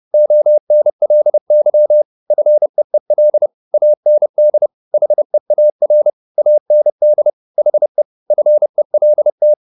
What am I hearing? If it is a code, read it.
Morse code, heard as a keyed tone, about 30 wpm, 600 Hz. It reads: ONLY FEEL AND HEAR AND HE FELT